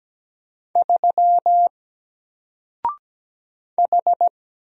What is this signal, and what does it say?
Morse code, heard as a keyed tone, about 17 wpm, 700 Hz. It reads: 3 H
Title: Single Character (Letter:Number:Punctuation:Prosign) Alt 17wpm